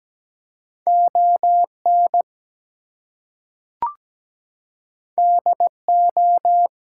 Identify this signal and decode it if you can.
Morse code, heard as a keyed tone, about 17 wpm, 700 Hz. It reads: ON DO
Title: Two Letter Words - 17wpm